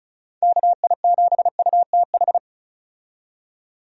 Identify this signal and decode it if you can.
Morse code, heard as a keyed tone, about 36 wpm, 700 Hz. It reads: KI7UTH
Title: USA Call Signs - 6 Characters - 35wpm